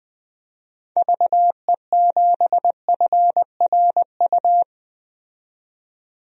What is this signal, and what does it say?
Morse code, heard as a keyed tone, about 21 wpm, 700 Hz. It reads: VE7FRU
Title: DXCC-100 Call Signs - 6 Characters - 20wpm